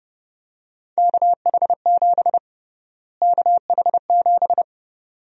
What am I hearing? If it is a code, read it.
Morse code, heard as a keyed tone, about 30 wpm, 700 Hz. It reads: KH7 KH7
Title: USA Call Sign Prefixes - Repeated 2x - 30wpm